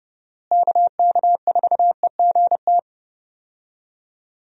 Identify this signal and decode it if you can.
Morse code, heard as a keyed tone, about 29 wpm, 700 Hz. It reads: KK4EGT
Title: USA Call Signs - 6 Characters - Repeated 3x - 30wpm